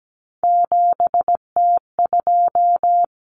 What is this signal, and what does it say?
Morse code, heard as a keyed tone, about 17 wpm, 690 Hz. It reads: 7T2
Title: Common Cut Numbers - 3-Digit 17wpm